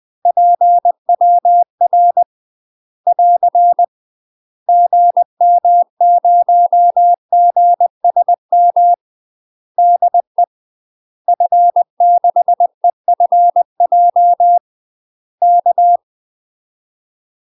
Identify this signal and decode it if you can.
Morse code, heard as a keyed tone, about 20 wpm, 700 Hz. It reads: PWR <AR> GM0GSM DE F6EFJ K
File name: UK MASTER QSOs TRAINING / 20WPM / UK MASTER QSOs 0022